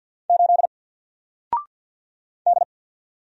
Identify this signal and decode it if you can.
Morse code, heard as a keyed tone, about 48 wpm, 700 Hz. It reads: 8 D
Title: Single Character (Letter:Number:Punctuation:Prosign) 50wpm